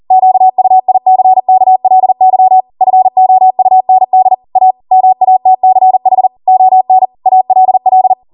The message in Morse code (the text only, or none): QUICKLY FOUND A MATCH ON ALL